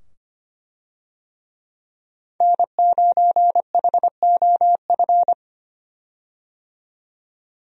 Morse code, heard as a keyed tone, about 25 wpm, 700 Hz. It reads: N9HOF
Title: USA Call Signs - 5 Characters - 25wpm